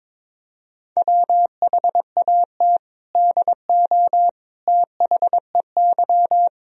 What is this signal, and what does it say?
Morse code, heard as a keyed tone, about 22 wpm, 700 Hz. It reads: WHAT DO THEY
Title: Sentences from Top 1000 Words - Review Part 3 - 22wpm